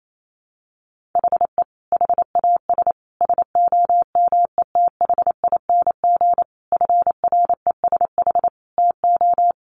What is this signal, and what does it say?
Morse code, heard as a keyed tone, about 28 wpm, 700 Hz. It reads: HE HAS SOMETHING FRESH TO
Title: Sentences from Top 1200 Words 28wpm